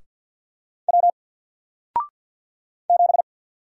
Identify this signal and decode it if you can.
Morse code, heard as a keyed tone, about 52 wpm, 700 Hz. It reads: W 7
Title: Single Character (Letter:Number:Punctuation:Prosign) Alt 50wpm